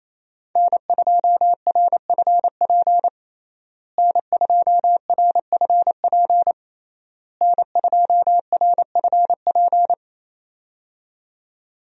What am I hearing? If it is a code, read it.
Morse code, heard as a keyed tone, about 28 wpm, 700 Hz. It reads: N2RFP N2RFP N2RFP
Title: USA Call Signs - 5 Characters - Repeated 3x - 28wpm